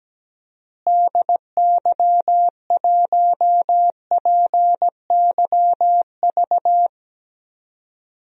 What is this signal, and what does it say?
Morse code, heard as a keyed tone, about 17 wpm, 700 Hz. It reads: DY1PYV
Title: DXCC-100 Call Signs - 6 Characters - Repeated 3x - 17wpm